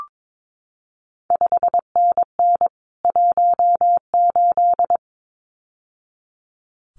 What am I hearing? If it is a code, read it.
Morse code, heard as a keyed tone, about 22 wpm, 700 Hz. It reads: 5NN 18
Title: CQ WW DX Contest Exchange Encore - 22wpm